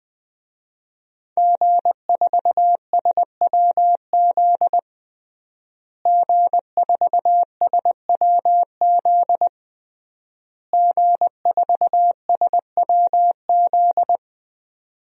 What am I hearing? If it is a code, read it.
Morse code, heard as a keyed tone, about 20 wpm, 700 Hz. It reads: G4SWZ G4SWZ G4SWZ
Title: DXCC-100 Call Signs - 5 Characters - Repeated 3x - 20wpm